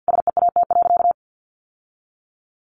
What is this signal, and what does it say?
Morse code, heard as a keyed tone, about 50 wpm, 700 Hz. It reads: SEAT.
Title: Sentences from Top 1500 Words 50wpm